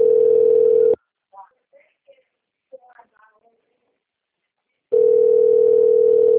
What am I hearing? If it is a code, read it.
Ringback tone (North America), 440 + 480 Hz, with speech.
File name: out- - - - - .9428